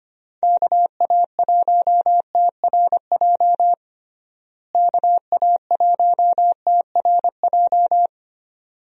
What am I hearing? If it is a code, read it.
Morse code, heard as a keyed tone, about 25 wpm, 700 Hz. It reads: KA1TRJ KA1TRJ
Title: USA Call Signs - 6 Characters - Repeated 2x - 25wpm